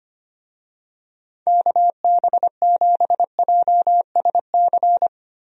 Morse code, heard as a keyed tone, about 25 wpm, 700 Hz. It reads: KB7JSC
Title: USA Call Signs - 6 Characters - Repeated 2x - 25wpm